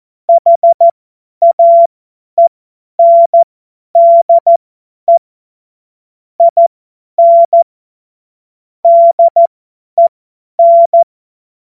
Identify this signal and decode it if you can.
Morse code, heard as a keyed tone, about 14 wpm, spaced as for 10 wpm, 680 Hz. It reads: HAENDE IN DEN
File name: Morsekurs / Witze / Witz 22. August 2023 50 F70